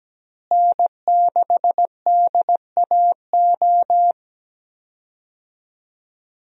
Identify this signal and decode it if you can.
Morse code, heard as a keyed tone, about 17 wpm, 700 Hz. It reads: N6DAO
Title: USA Call Signs - 5 Characters - Repeated 2x - 17wpm